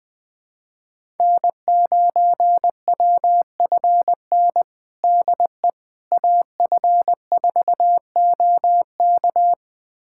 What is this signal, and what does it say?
Morse code, heard as a keyed tone, about 20 wpm, 700 Hz. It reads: N9WFN DE AF4OK
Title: USA Callsign DE Callsign - 5 Characters - 20wpm